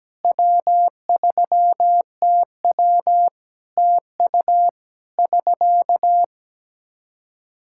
Morse code, heard as a keyed tone, about 17 wpm, 700 Hz. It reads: W3TW TU <SK>
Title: Practice QSO - Exchange 4 - Ending - 17wpm